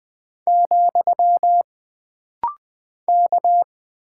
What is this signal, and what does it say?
Morse code, heard as a keyed tone, about 20 wpm, 700 Hz. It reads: , K
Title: Single Character (Letter:Number:Punctuation:Prosign) - Rapid-Fire - 20wpm